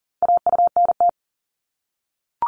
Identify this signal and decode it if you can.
Morse code, heard as a keyed tone, about 41 wpm, 700 Hz. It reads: AUNT
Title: Multi-letter Phonemes 40wpm